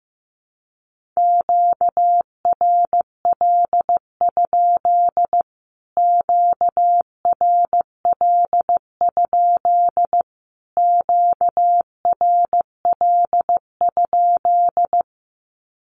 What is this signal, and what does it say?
Morse code, heard as a keyed tone, about 15 wpm, 700 Hz. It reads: QRL? QRL? QRL?
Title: Q-Codes Repeated 3x 15wpm